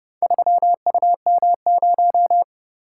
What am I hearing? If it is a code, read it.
Morse code, heard as a keyed tone, about 29 wpm, 700 Hz. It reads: 3UM0